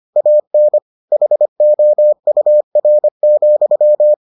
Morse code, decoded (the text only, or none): AN HOUR,